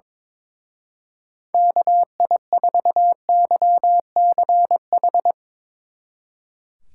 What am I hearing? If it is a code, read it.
Morse code, heard as a keyed tone, about 24 wpm, 700 Hz. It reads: KI4YCH
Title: Speed Racing - USA 2x3 Call Signs 22wpm